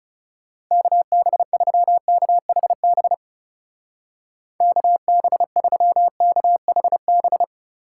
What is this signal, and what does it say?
Morse code, heard as a keyed tone, about 34 wpm, 700 Hz. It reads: KB3KHB KB3KHB
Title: Speed Racing - USA Call Signs 30wpm